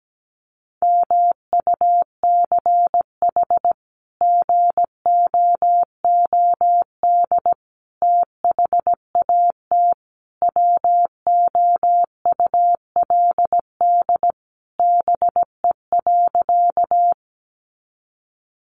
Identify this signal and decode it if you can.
Morse code, heard as a keyed tone, about 17 wpm, 700 Hz. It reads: MUCH GOOD THAT WOULD BE.
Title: Sentences from Top 200 Words 17wpm